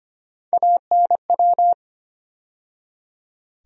Morse code, heard as a keyed tone, about 25 wpm, 700 Hz. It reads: ANW
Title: USA Call Sign Postfixes - 3 Letters 25wpm